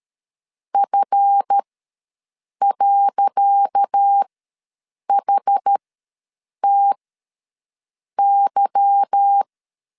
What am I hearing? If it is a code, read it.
Morse code, heard as a keyed tone, about 13 wpm, spaced as for 6 wpm, 800 Hz. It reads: F.HTY
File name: LEZIONI / 10 SIGNS / 10C DOT ws 12wpm 3548